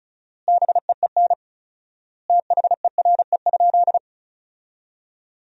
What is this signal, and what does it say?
Morse code, heard as a keyed tone, about 34 wpm, 700 Hz. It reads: BEEN THERE?
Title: 4x Word-Spacing - Sentences from Top 100 Words - 35wpm